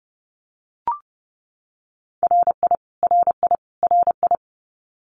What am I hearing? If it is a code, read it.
Morse code, heard as a keyed tone, about 30 wpm, 700 Hz. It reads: RI RI RI
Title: US States Repeated 3x 30wpm